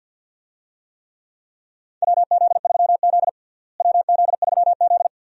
Morse code, heard as a keyed tone, about 53 wpm, 700 Hz. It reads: WZ3Z WZ3Z
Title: USA Call Signs - 4 Characters - Repeated 2x - 50wpm